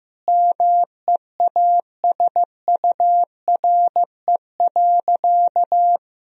Morse code, heard as a keyed tone, about 15 wpm, 700 Hz. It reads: MEASURE.
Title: Sentences from Top 1000 Words - Review Part 2 - 15wpm